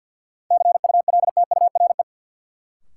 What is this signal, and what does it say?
Morse code, heard as a keyed tone, about 53 wpm, 700 Hz. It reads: CULTURE
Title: Sentences from Top 1000 Words - Review Part 6 - New Word Drill - 50wpm